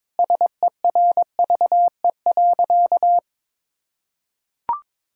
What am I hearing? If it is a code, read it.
Morse code, heard as a keyed tone, about 22 wpm, 700 Hz. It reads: SERVE.
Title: Sentences from Top 1000 Words - Review Part 4 - 22wpm